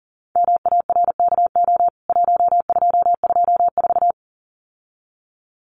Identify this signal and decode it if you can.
Morse code, heard as a keyed tone, about 40 wpm, 700 Hz. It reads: MARKO 1224
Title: CWT Exchange 40wpm